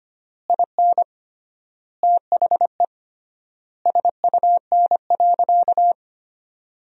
Morse code, heard as a keyed tone, about 25 wpm, 700 Hz. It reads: IN THE SUN.
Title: 3x Word-Spacing - Sentences from Top 300 Words - 25wpm